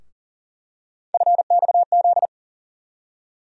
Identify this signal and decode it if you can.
Morse code, heard as a keyed tone, about 40 wpm, 700 Hz. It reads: FXZ
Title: USA Call Sign Postfixes - 3 Letters 40wpm